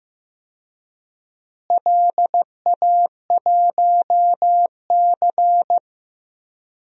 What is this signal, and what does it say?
Morse code, heard as a keyed tone, about 15 wpm, 700 Hz. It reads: LA1C
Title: DXCC-100 Call Signs - 4 Characters - Repeated 3x - 15wpm